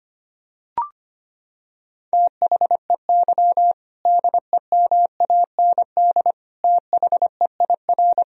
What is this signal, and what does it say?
Morse code, heard as a keyed tone, about 25 wpm, 700 Hz. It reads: THEY DEMAND THEIR
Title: Sentences from Top 1000 Words - Review Part 3 - 25wpm